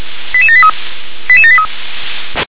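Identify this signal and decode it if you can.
Five-tone selective call (ZVEI), heard as quick runs of five tones, tones of about 70 ms. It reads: 88783 88783
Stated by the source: ZVEI five-tone selective call received on an SDR.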